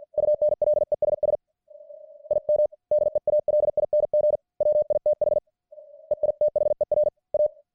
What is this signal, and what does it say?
Morse code, heard as a keyed tone, about 50 wpm, 600 Hz. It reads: UNLESS IM DEALING WITH EITHER A